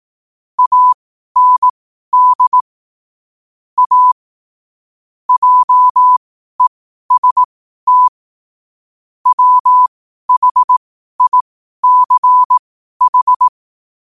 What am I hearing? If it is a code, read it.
Morse code, heard as a keyed tone, about 18 wpm, spaced as for 12 wpm, 1000 Hz. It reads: AND A JEST WHICH
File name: qotdfiles / 12wpm